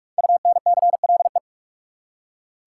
Morse code, heard as a keyed tone, about 46 wpm, 700 Hz. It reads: UNCLE
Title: Five Letter Words - 45wpm